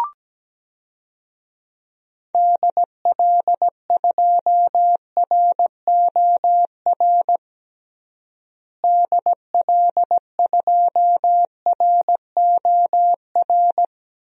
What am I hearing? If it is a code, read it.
Morse code, heard as a keyed tone, about 17 wpm, 700 Hz. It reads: DL2ROR DL2ROR
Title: DXCC-100 Call Signs - 6 Characters - Repeated 2x - 17wpm